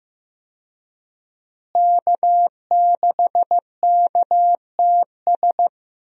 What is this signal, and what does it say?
Morse code, heard as a keyed tone, about 15 wpm, 700 Hz. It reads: K6KTS
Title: USA Call Signs - 5 Characters - Repeated 2x - 15wpm